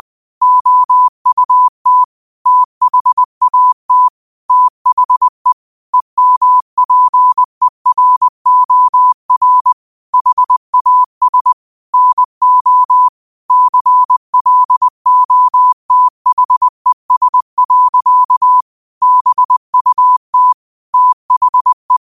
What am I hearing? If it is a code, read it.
Morse code, heard as a keyed tone, about 20 wpm, 1000 Hz. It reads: OUT THAT THE EMPEROR HAS NO CLOTHES. BUT THE
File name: qotdfiles / 20wpm